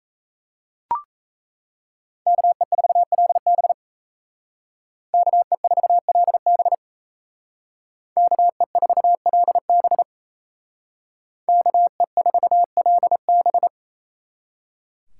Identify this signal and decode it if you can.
Morse code, heard as a keyed tone, about 36 wpm, 700 Hz. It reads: KE4LB KE4LB KE4LB KE4LB